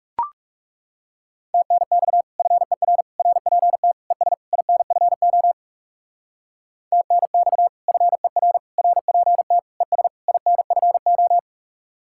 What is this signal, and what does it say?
Morse code, heard as a keyed tone, about 36 wpm, 700 Hz. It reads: TNX FER RPT ES INFO TNX FER RPT ES INFO
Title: Speed-Racing Common QSO Phrases - 30wpm